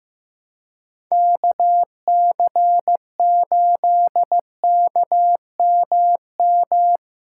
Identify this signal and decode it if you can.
Morse code, heard as a keyed tone, about 15 wpm, 700 Hz. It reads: KC8KMM
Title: USA Callsign DE Callsign 15wpm